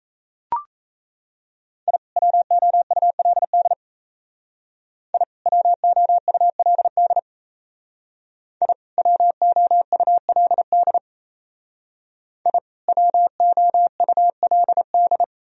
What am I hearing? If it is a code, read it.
Morse code, heard as a keyed tone, about 36 wpm, 700 Hz. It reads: I WOULD I WOULD I WOULD I WOULD